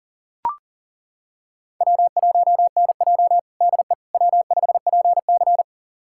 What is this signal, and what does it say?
Morse code, heard as a keyed tone, about 37 wpm, 700 Hz. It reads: W1NJ DE W5PC
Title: USA Callsign DE Callsign - 4 Characters - 40wpm